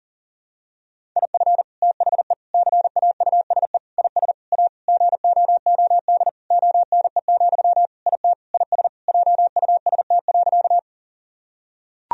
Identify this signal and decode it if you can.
Morse code, heard as a keyed tone, about 40 wpm, 700 Hz. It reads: IF THE CAUSE IS A GOOD ONE, IT IS JUST.
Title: Sentences from Top 1000 Words - Review Part 1 - 40wpm